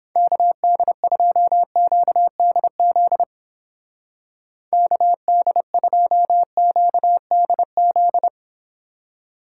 Morse code, heard as a keyed tone, about 31 wpm, 700 Hz. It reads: KD2QDZ KD2QDZ